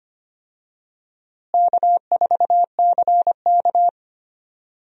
Morse code, heard as a keyed tone, about 25 wpm, 700 Hz. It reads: K4CK
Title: USA Call Signs - 4 Characters - Repeated 2x - 25wpm